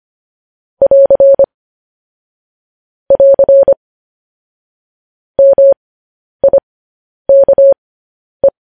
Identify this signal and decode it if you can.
Morse code, heard as a keyed tone, about 25 wpm, spaced as for 10 wpm, 550 Hz. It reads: <AR> <AR> MIKE
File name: podcast / News headlines-1x-25x10